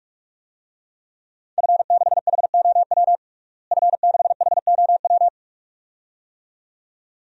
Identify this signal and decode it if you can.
Morse code, heard as a keyed tone, about 48 wpm, 700 Hz. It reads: F6HOW F6HOW
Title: DXCC-100 Call Signs - 5 Characters - Repeated 2x - 45wpm